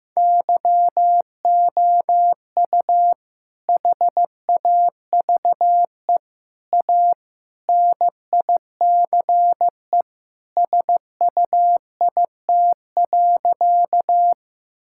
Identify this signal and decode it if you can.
Morse code, heard as a keyed tone, about 15 wpm, 700 Hz. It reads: YOU HAVE A NICE SUIT.
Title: Sentences from Top 1600 Words 15wpm